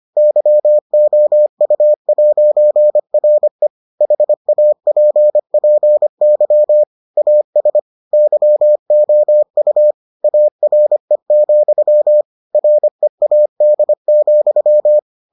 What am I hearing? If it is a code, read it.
Morse code, heard as a keyed tone, about 25 wpm, 600 Hz. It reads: YOU'RE HAPPY AS YOU ARE, READ,